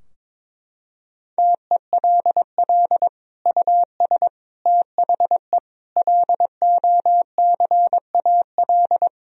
Morse code, heard as a keyed tone, about 22 wpm, 700 Hz. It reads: TELL US THE LOCAL
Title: Sentences from Top 1000 Words - Review Part 3 - 22wpm